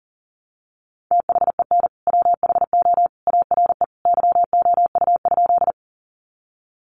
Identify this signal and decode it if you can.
Morse code, heard as a keyed tone, about 40 wpm, 700 Hz. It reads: THEN WHO ARE YOU?
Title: Sentences from Top 100 Words 40wpm